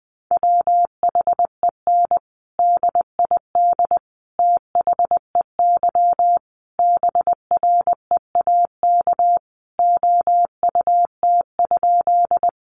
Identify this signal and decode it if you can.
Morse code, heard as a keyed tone, about 20 wpm, 700 Hz. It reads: WHEN DID THEY BREAK OUT?
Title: Sentences from Top 900 Words 20wpm